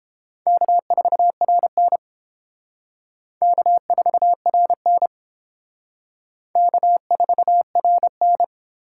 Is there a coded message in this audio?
Yes: Morse code from 0.5 to 8.4 s, heard as a keyed tone, about 28 wpm, 700 Hz. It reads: K4RN K4RN K4RN